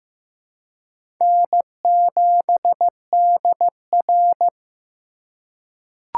Morse code, heard as a keyed tone, about 16 wpm, 700 Hz. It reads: N7DR
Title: K1USN SST QSO Practice - 15wpm